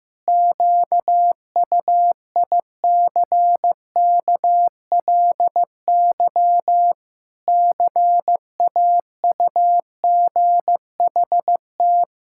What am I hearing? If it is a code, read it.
Morse code, heard as a keyed tone, about 15 wpm, 700 Hz. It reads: QUICKLY CAUGHT
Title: Pangram Challenge v2 - 15wpm